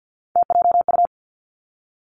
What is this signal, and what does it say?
Morse code, heard as a keyed tone, about 50 wpm, 700 Hz. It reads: TJU
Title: Three Challenging Letter-Number Combinations 50wpm